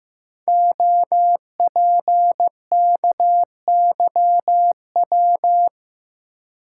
Morse code, heard as a keyed tone, about 15 wpm, 700 Hz. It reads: OPKYW
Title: Code Groups - 5 Letter:Numbers - 15wpm